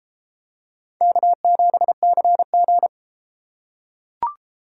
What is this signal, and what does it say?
Morse code, heard as a keyed tone, about 33 wpm, 700 Hz. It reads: K7CG